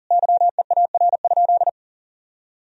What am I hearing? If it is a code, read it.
Morse code, heard as a keyed tone, about 37 wpm, 700 Hz. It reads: YEAR?
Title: Sentences from Top 1000 Words - Review Part 4 - 40wpm